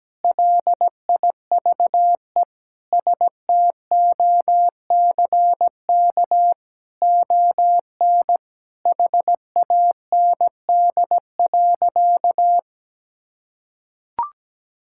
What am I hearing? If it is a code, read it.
Morse code, heard as a keyed tone, about 17 wpm, 700 Hz. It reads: LIVE STOCK ON HAND.
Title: Sentences from Top 1000 Words - Review Part 5 - 17wpm